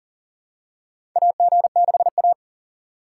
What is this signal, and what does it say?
Morse code, heard as a keyed tone, about 40 wpm, 700 Hz. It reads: AG6A
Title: USA Call Signs - 4 Characters - 3x Character Spacing - 40wpm